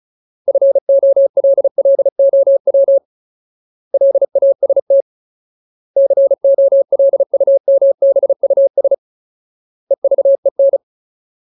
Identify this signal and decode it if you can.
Morse code, heard as a keyed tone, about 34 wpm, 550 Hz. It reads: FOLLOW LAST COLUMBUS EVEN
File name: Ultimate Word List / 35WPM Ultimate